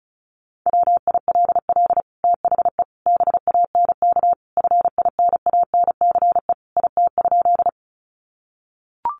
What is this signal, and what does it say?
Morse code, heard as a keyed tone, about 34 wpm, 700 Hz. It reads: WILL THE BANK FINANCE IT?
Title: Sentences from Top 1500 Words 35wpm